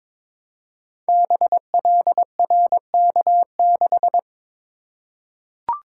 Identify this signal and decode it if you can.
Morse code, heard as a keyed tone, about 22 wpm, 700 Hz. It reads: BLRK6
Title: Code Groups - 5 Letter:Numbers - 22wpm